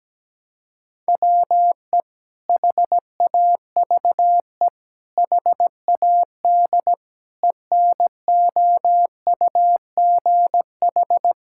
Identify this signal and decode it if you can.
Morse code, heard as a keyed tone, about 17 wpm, 700 Hz. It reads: WE HAVE HAD ENOUGH
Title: Sentences from Top 1000 Words - Review Part 5 - 17wpm